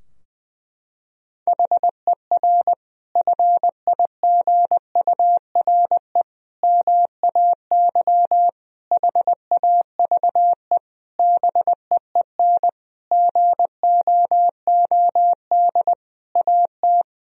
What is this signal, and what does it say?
Morse code, heard as a keyed tone, about 20 wpm, 700 Hz. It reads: HER FIGURE MAY HAVE BEEN GOOD AT
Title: Sentences from Top 1000 Words - Review Part 2 - 20wpm